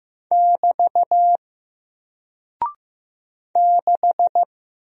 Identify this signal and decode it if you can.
Morse code, heard as a keyed tone, about 15 wpm, 700 Hz. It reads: <BT> 6
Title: Single Character (Letter:Number:Punctuation:Prosign) Alt 15wpm